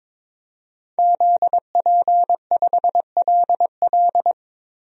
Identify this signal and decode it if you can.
Morse code, heard as a keyed tone, about 22 wpm, 700 Hz. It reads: ZP5LL
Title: DXCC-100 Call Signs - 5 Characters - Repeated 3x - 22wpm